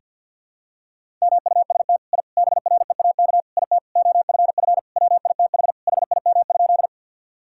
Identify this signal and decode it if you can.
Morse code, heard as a keyed tone, about 46 wpm, 700 Hz. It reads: MUST I BREAK IT OFF WITH HIM?
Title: Sentences from Top 1000 Words - Review Part 6 - 50wpm